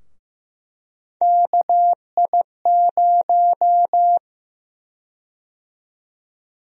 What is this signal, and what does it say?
Morse code, heard as a keyed tone, about 15 wpm, 700 Hz. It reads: KI0
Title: USA Call Sign Prefixes - Repeated 2x - 15wpm